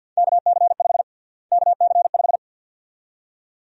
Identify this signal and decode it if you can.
Morse code, heard as a keyed tone, about 48 wpm, 700 Hz. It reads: KC5 KC5
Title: USA Call Sign Prefixes - Repeated 2x - 50wpm